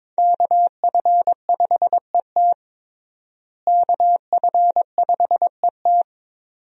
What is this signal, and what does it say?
Morse code, heard as a keyed tone, about 22 wpm, 700 Hz. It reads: KF5ET KF5ET
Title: USA Call Signs - 5 Characters - Repeated 2x - 22wpm